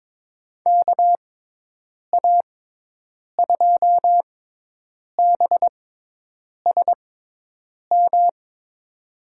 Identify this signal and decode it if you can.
Morse code, heard as a keyed tone, about 22 wpm, spaced as for 8 wpm, 700 Hz. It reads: KA2BSM
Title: USA Call Signs - 6x Character Spacing - 22wpm